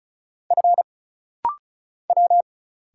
Morse code, heard as a keyed tone, about 36 wpm, 700 Hz. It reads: F W
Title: Single Letters - Rapid-Fire 35wpm